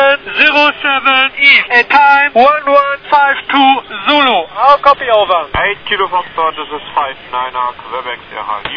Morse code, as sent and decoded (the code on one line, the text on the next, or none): none